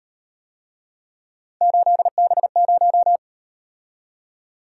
Morse code, heard as a keyed tone, about 37 wpm, 700 Hz. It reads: 8B0